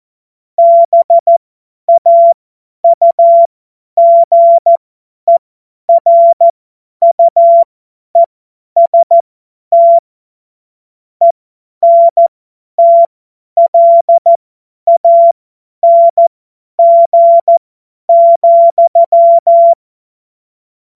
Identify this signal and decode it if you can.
Morse code, heard as a keyed tone, about 14 wpm, spaced as for 10 wpm, 680 Hz. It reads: BAUGERUEST ENTLANG,
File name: Morsekurs / Witze / Witz 31.Oktober 2023 50 F70